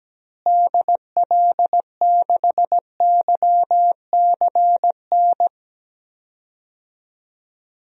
Morse code, heard as a keyed tone, about 17 wpm, 700 Hz. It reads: DL6YCN
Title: DXCC-100 Call Signs - 6 Characters - Repeated 3x - 17wpm